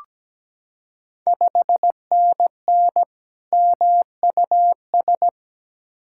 Morse code, heard as a keyed tone, about 17 wpm, 700 Hz. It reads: 5NN MUS
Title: Ontario QSO Party Exchange 17wpm